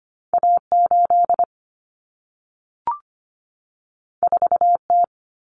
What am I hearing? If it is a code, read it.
Morse code, heard as a keyed tone, about 25 wpm, 700 Hz. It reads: A8 4T
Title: Common Cut Numbers - 2-Digit 25wpm